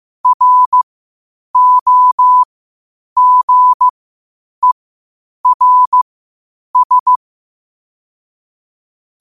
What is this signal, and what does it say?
Morse code, heard as a keyed tone, about 15 wpm, spaced as for 8 wpm, 1000 Hz. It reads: ROGERS
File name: qotdfiles / 8wpm